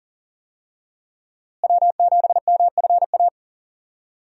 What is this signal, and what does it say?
Morse code, heard as a keyed tone, about 40 wpm, 700 Hz. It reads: W7MFA
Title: USA Call Signs - 5 Characters - Repeated 2x - 40wpm